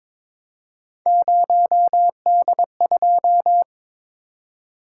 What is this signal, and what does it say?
Morse code, heard as a keyed tone, about 22 wpm, 700 Hz. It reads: 0D2
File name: practice / 22wpm / Three Letter-Number Combinations - 22wpm